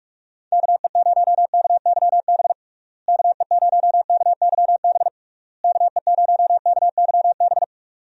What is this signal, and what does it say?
Morse code, heard as a keyed tone, about 46 wpm, 700 Hz. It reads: KE0KYB KE0KYB KE0KYB
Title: USA Call Signs - 6 Characters - Repeated 3x - 45wpm